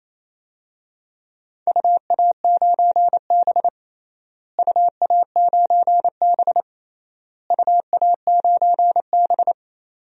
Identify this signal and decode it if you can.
Morse code, heard as a keyed tone, about 28 wpm, 700 Hz. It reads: UA9B UA9B UA9B
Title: DXCC-100 Call Signs - 4 Characters - Repeated 3x - 28wpm